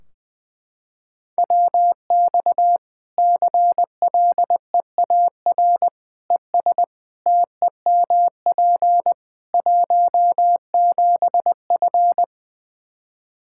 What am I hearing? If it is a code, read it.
Morse code, heard as a keyed tone, about 20 wpm, 700 Hz. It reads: WX CLEAR ES TEMP 17F
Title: Practice QSO - Exchange 2 - Info - 20wpm